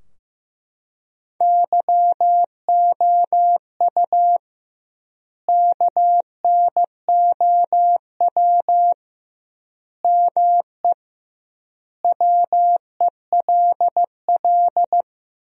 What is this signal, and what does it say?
Morse code, heard as a keyed tone, about 16 wpm, 700 Hz. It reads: YOU KNOW ME WELL
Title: 2x Word-Spacing - Sentences from Top 400 Words - 15wpm